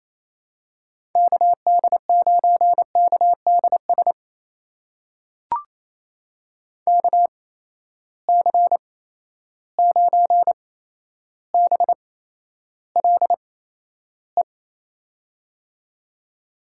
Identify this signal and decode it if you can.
Morse code, heard as a keyed tone, about 28 wpm, spaced as for 12 wpm, 700 Hz. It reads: KD9KDS KC9BLE
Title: USA Call Signs - 6 Characters - 8x Character Spacing - 28wpm